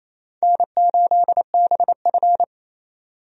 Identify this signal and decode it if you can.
Morse code, heard as a keyed tone, about 28 wpm, 700 Hz. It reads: N8BF
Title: USA Call Signs - 4 Characters - 28wpm